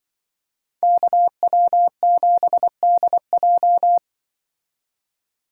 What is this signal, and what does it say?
Morse code, heard as a keyed tone, about 24 wpm, 700 Hz. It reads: KW7DJ
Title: Speed Racing - USA 2x2 Call Signs 20wpm